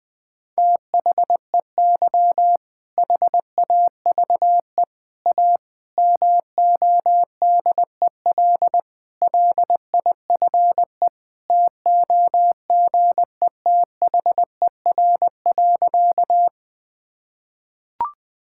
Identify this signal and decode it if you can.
Morse code, heard as a keyed tone, about 20 wpm, 700 Hz. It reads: THEY HAVE A MODEL LIFE TOGETHER.
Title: Sentences from Top 1000 Words - Review Part 5 - 20wpm